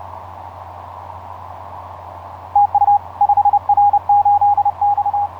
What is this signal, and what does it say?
Morse code, heard as a keyed tone, about 32 wpm, 820 Hz. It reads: TU 5R8X
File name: 5r8x iz5ebl etc 12m cw 07nov2010 1003